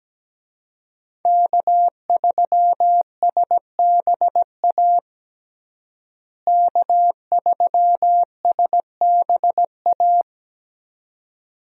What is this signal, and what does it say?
Morse code, heard as a keyed tone, about 17 wpm, 700 Hz. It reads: K3SBA K3SBA
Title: USA Call Signs - 5 Characters - Repeated 2x - 17wpm